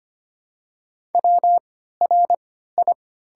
Morse code, heard as a keyed tone, about 25 wpm, spaced as for 14 wpm, 700 Hz. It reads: WRI
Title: 3x Character Spacing - Top 100 Words - 25wpm